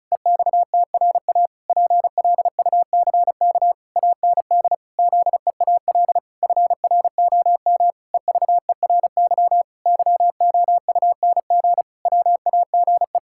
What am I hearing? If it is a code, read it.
Morse code, heard as a keyed tone, about 34 wpm, 700 Hz. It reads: EXTRA PLUCK AND ZEAL FROM EVERY YOUNG WAGE
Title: Pangram Challenge v2 - 35wpm